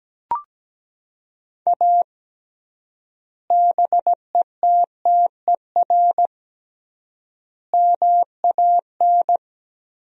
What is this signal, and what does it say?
Morse code, heard as a keyed tone, about 17 wpm, 700 Hz. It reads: A BETTER MAN
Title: 3x Word-Spacing - Sentences from Top 400 Words - 17wpm